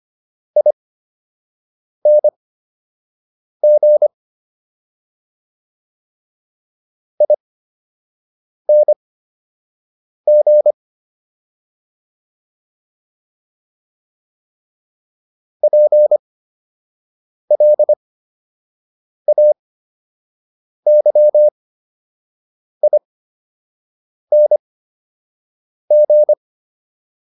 Morse code, heard as a keyed tone, about 25 wpm, spaced as for 6 wpm, 600 Hz. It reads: ING ING PLAYING